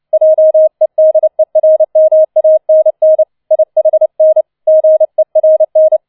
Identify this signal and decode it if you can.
Morse code, heard as a keyed tone, about 29 wpm, 630 Hz. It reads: JEDERMANN IHN GERN